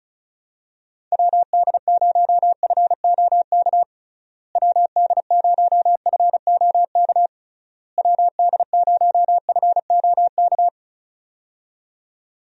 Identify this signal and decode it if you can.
Morse code, heard as a keyed tone, about 36 wpm, 700 Hz. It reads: WD0FOK WD0FOK WD0FOK
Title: USA Call Signs - 6 Characters - Repeated 3x - 35wpm